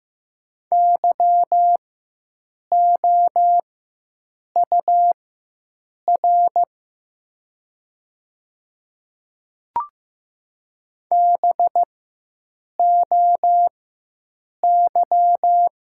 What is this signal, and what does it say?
Morse code, heard as a keyed tone, about 15 wpm, spaced as for 7 wpm, 700 Hz. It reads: YOUR BOY
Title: 4x Character Spacing - Top 300 Words - 15wpm